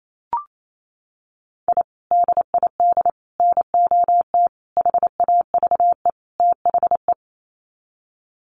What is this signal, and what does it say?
Morse code, heard as a keyed tone, about 28 wpm, 690 Hz. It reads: I DID NOT HAVE THE
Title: Sets of 5 Words Encore 28wpm